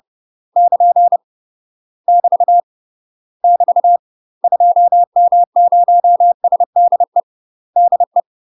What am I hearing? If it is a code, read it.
Morse code, heard as a keyed tone, about 30 wpm, 700 Hz. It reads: <KN> <BT> <BT> 2M0SDE DE
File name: UK MASTER QSOs TRAINING / 30WPM / UK MASTER QSOs 0027